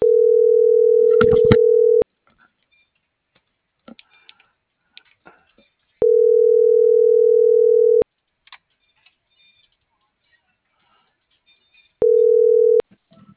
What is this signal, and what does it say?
Ringback tone (North America), 440 + 480 Hz.